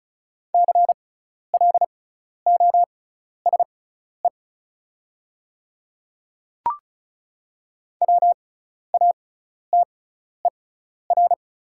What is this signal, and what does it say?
Morse code, heard as a keyed tone, about 36 wpm, spaced as for 12 wpm, 690 Hz. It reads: CLOSE WATER
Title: 6x Character Spacing - Top 300 Words - 35wpm